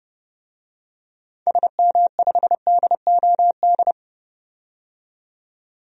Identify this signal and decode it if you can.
Morse code, heard as a keyed tone, about 30 wpm, 700 Hz. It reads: SM5DOD
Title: DXCC-100 Call Signs - 6 Characters - 30wpm